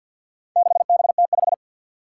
Morse code, heard as a keyed tone, about 52 wpm, 700 Hz. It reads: 6BT5
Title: Four Challenging Letter-Number Combinations 50wpm